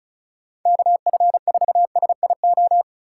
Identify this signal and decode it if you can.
Morse code, heard as a keyed tone, about 36 wpm, 700 Hz. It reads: KF4SIO
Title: USA Call Signs - 6 Characters - 2x Character Spacing - 35wpm